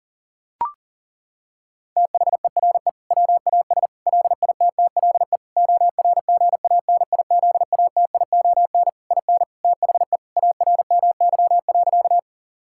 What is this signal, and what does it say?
Morse code, heard as a keyed tone, about 40 wpm, 700 Hz. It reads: THERE WAS LITTLE ORGANIZATION IN THE ARMY.
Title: Sentences from Top 1000 Words - Review Part 5 - 40wpm